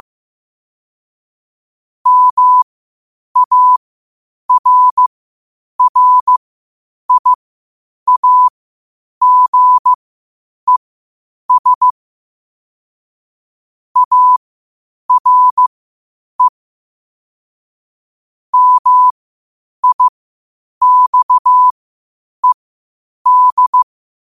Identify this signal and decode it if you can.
Morse code, heard as a keyed tone, about 15 wpm, spaced as for 8 wpm, 1000 Hz. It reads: MARRIAGES ARE MIXED
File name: 8wpm